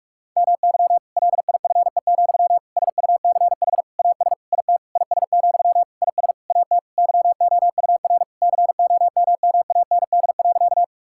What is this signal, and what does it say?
Morse code, heard as a keyed tone, about 46 wpm, 700 Hz. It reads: MY LIFE, SUCH AS IT IS, IS AT YOUR COMMAND.
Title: Sentences from Top 1600 Words 45wpm